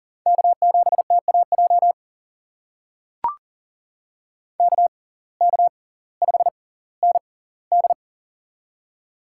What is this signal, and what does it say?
Morse code, heard as a keyed tone, about 40 wpm, spaced as for 20 wpm, 700 Hz. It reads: K7TAJ KK5ND
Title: USA Call Signs - 5 Characters - 6x Character Spacing - 40wpm